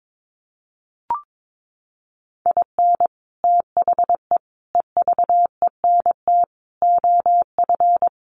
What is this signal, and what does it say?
Morse code, heard as a keyed tone, about 22 wpm, 700 Hz. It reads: IN THE EVENT OF
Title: Sets of 5 Words Encore 22wpm